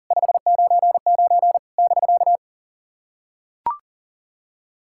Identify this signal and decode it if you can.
Morse code, heard as a keyed tone, about 40 wpm, 700 Hz. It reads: 599 <BK>
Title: SOTA QSO Practice 40wpm